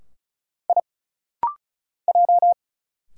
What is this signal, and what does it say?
Morse code, heard as a keyed tone, about 34 wpm, 700 Hz. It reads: I J
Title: Single Letters - Rapid-Fire 35wpm